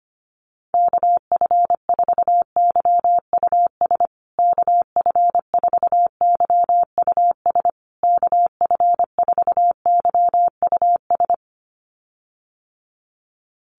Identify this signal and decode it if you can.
Morse code, heard as a keyed tone, about 25 wpm, 700 Hz. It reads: KF4YUS KF4YUS KF4YUS
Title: USA Call Signs Repeated 3x 25wpm